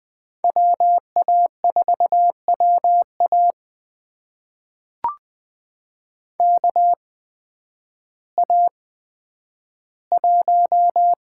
Morse code, heard as a keyed tone, about 20 wpm, spaced as for 11 wpm, 700 Hz. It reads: WA4WA KA1
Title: USA Call Signs - 8x Character Spacing - 20wpm